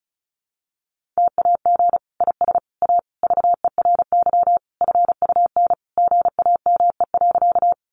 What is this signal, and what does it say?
Morse code, heard as a keyed tone, about 34 wpm, 700 Hz. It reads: TAG IS A VERY FUN GAME.
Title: Additional Easy Phrases 35wpm